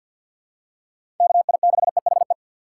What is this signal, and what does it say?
Morse code, heard as a keyed tone, about 46 wpm, 700 Hz. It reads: KI6EHE
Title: USA Call Signs - 6 Characters - 6x Character Spacing - 50wpm